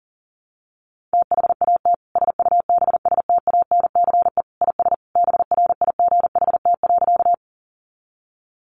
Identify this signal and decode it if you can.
Morse code, heard as a keyed tone, about 40 wpm, 700 Hz. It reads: THAT SUBSTANCE IS BRIGHT.
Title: Sentences from Top 1500 Words 40wpm